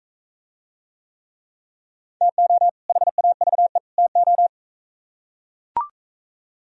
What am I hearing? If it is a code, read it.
Morse code, heard as a keyed tone, about 42 wpm, 700 Hz. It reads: TO HAVE TO